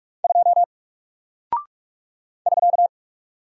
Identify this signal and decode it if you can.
Morse code, heard as a keyed tone, about 46 wpm, 700 Hz. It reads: 2 <SK>
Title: Single Character (Letter:Number:Punctuation:Prosign) Alt 45wpm